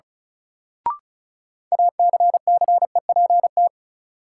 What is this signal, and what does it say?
Morse code, heard as a keyed tone, about 34 wpm, 700 Hz. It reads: ACCEPT
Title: Sentences from Top 1300 Words - New Word Drill Alt - 35wpm